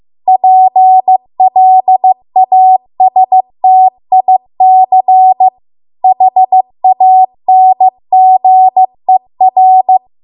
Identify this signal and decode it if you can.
Morse code, heard as a keyed tone, about 15 wpm, 750 Hz. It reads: PLASTIC HANGER